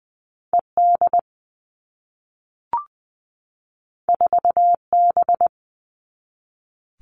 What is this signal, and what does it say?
Morse code, heard as a keyed tone, about 20 wpm, 700 Hz. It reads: ED 4B
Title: All Cut Numbers - 2-Digit 20wpm